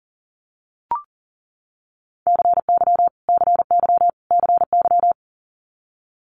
Morse code, heard as a keyed tone, about 41 wpm, 700 Hz. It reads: CY CY CY
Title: Multi-letter Phonemes Repeated 3x 40wpm